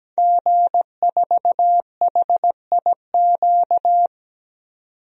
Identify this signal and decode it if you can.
Morse code, heard as a keyed tone, about 17 wpm, 700 Hz. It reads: G4HIQ
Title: DXCC-100 Call Signs - 5 Characters - 17wpm